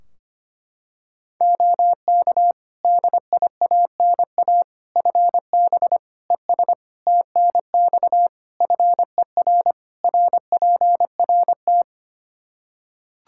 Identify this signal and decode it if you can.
Morse code, heard as a keyed tone, about 25 wpm, 700 Hz. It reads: OK DIANA FB ES TNX FER RPRT
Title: Practice QSO - Exchange 2 - Info - 25wpm